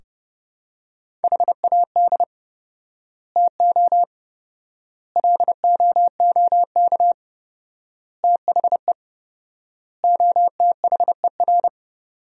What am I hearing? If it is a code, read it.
Morse code, heard as a keyed tone, about 30 wpm, 700 Hz. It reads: HAD TO LOOK THE OTHER
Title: 4x Word-Spacing - Sentences from Top 100 Words - 30wpm